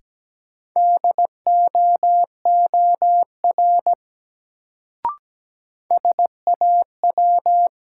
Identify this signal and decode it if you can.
Morse code, heard as a keyed tone, about 17 wpm, 700 Hz. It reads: DOOR SAW
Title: Sentences from Top 300 Words - New Word Drill - 17wpm